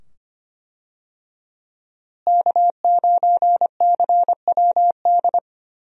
Morse code, heard as a keyed tone, about 25 wpm, 700 Hz. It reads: K9CWD
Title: USA Call Signs - 5 Characters - Repeated 2x - 25wpm